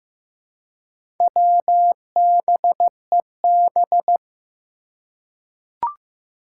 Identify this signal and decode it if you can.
Morse code, heard as a keyed tone, about 16 wpm, 700 Hz. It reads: WBEB
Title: Four Challenging Letter-Number Combinations 15wpm